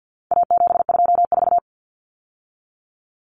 Morse code, heard as a keyed tone, about 50 wpm, 700 Hz. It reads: A724
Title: Common Cut Numbers - 4-Digit 50wpm